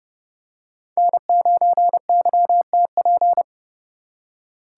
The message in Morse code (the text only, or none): N9YTP